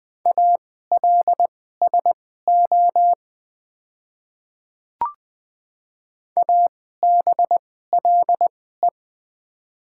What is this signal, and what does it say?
Morse code, heard as a keyed tone, about 20 wpm, spaced as for 14 wpm, 700 Hz. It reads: ALSO ABLE
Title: 2x Character Spacing - Top 500 Words - 20wpm